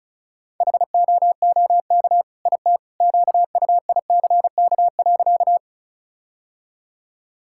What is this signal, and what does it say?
Morse code, heard as a keyed tone, about 34 wpm, 700 Hz. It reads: HOOK IT QUICK.
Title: Sentences from Top 1600 Words 35wpm